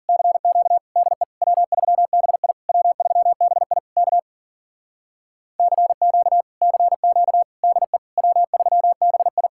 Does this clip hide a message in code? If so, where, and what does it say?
Yes: Morse code from 0.1 to 9.5 s, heard as a keyed tone, about 52 wpm, 700 Hz. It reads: CQ DE W3BI W3BI K CQ CQ DE W3BI